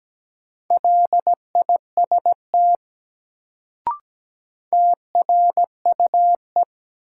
Morse code, heard as a keyed tone, about 17 wpm, 700 Hz. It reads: LIST TRUE
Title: Sentences from Top 400 Words - New Word Drill - 17wpm